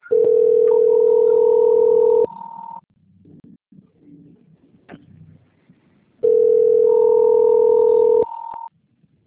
Ringback tone (North America), 440 + 480 Hz.